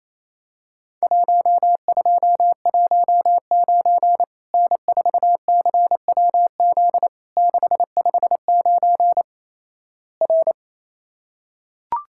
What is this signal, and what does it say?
Morse code, heard as a keyed tone, about 28 wpm, 700 Hz. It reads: 1219 N4CWZ 659 R
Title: Worked All Europe DX Contest - QTC Exchange - 28wpm